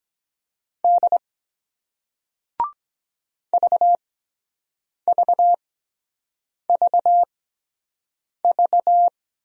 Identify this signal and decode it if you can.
Morse code, heard as a keyed tone, about 23 wpm, spaced as for 7 wpm, 700 Hz. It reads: D VVVV